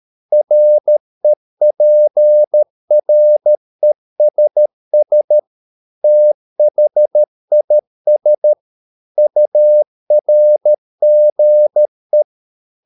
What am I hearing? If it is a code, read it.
Morse code, heard as a keyed tone, about 13 wpm, 600 Hz. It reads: REPRESS THIS URGE